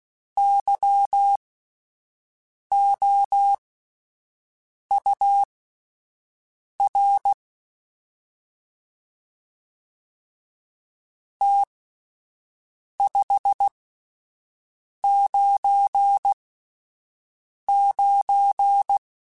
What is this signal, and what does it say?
Morse code, heard as a keyed tone, about 16 wpm, spaced as for 5 wpm, 800 Hz. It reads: YOUR T 599
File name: (spaced) / QSOs Spaced-mp3s / WA7QOX 1 @10 sp